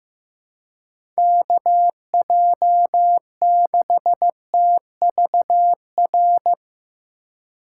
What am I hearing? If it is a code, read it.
Morse code, heard as a keyed tone, about 15 wpm, 700 Hz. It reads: KJ6TVR
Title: USA Call Signs - 6 Characters - 3x Character Spacing - 15wpm